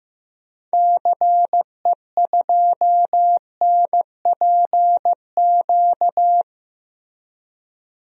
Morse code, heard as a keyed tone, about 15 wpm, 700 Hz. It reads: CE2NPQ
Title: DXCC-100 Call Signs - 6 Characters - Repeated 3x - 15wpm